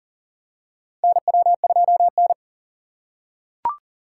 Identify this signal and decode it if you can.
Morse code, heard as a keyed tone, about 40 wpm, 700 Hz. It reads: NW2N
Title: USA Call Signs - 4 Characters - 4x Character Spacing - 40wpm